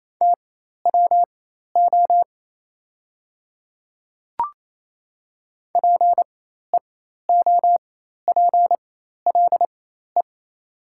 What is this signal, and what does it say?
Morse code, heard as a keyed tone, about 28 wpm, spaced as for 13 wpm, 700 Hz. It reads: TWO PEOPLE
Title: 4x Character Spacing - Top 100 Words - 28wpm